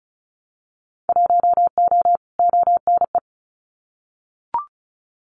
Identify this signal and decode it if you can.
Morse code, heard as a keyed tone, about 36 wpm, 700 Hz. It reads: 1O ONE
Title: Winter Field Day Exchanges 35wpm